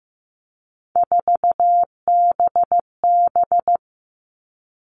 Morse code, heard as a keyed tone, about 15 wpm, 700 Hz. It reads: 4BB
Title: All Cut Numbers - 3-Digit 15wpm